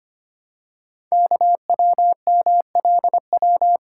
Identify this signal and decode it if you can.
Morse code, heard as a keyed tone, about 25 wpm, 700 Hz. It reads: KWMLW
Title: Code Groups - 5 Letters - 25wpm